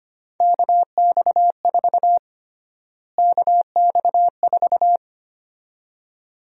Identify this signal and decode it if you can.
Morse code, heard as a keyed tone, about 25 wpm, 700 Hz. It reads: KX4 KX4
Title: USA Call Sign Prefixes - Repeated 2x - 25wpm